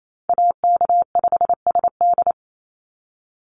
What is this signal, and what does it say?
Morse code, heard as a keyed tone, about 28 wpm, 700 Hz. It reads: AK5SD
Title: NA SOTA Chaser Callsigns 28wpm